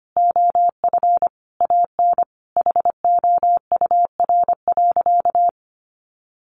Morse code, heard as a keyed tone, about 25 wpm, 700 Hz. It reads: OF AN HOUR.
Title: Sentences from Top 1500 Words 25wpm